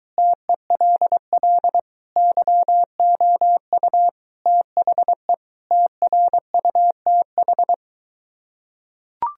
Morse code, heard as a keyed tone, about 23 wpm, 700 Hz. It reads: TELL YOU THE TRUTH